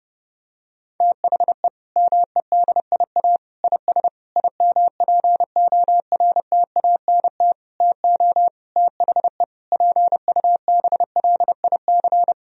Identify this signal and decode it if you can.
Morse code, heard as a keyed tone, about 30 wpm, 700 Hz. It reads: THE MEDIA IS IMPORTANT TO THE PUBLIC
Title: Sentences from Top 1000 Words - Review Part 4 - 30wpm